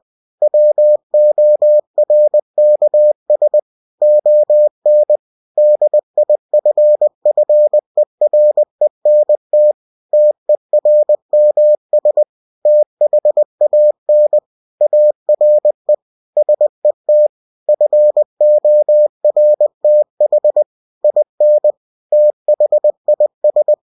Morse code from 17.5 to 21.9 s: FORTH IN